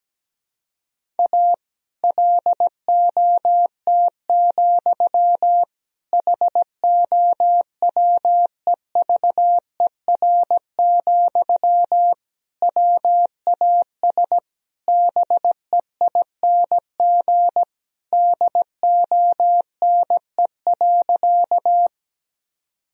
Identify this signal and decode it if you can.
Morse code, heard as a keyed tone, about 17 wpm, 700 Hz. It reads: A LOT, HOWEVER, WAS BEING DONE.
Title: Sentences from Top 1000 Words - Review Part 3 - 17wpm